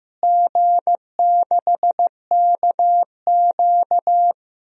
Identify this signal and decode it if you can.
Morse code, heard as a keyed tone, about 15 wpm, 700 Hz. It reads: G6KQ
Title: DXCC-100 Call Signs - 4 Characters - Repeated 3x - 15wpm